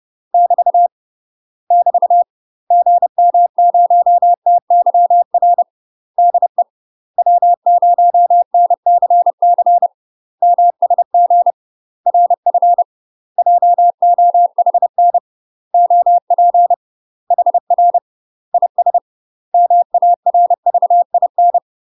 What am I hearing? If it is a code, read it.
Morse code, heard as a keyed tone, about 30 wpm, 700 Hz. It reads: <BT> <BT> GM0TYR DE W0NCC MSG RF JOHN OP HR IS MARVIN
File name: UK MASTER QSOs TRAINING / 30WPM / UK MASTER QSOs 0016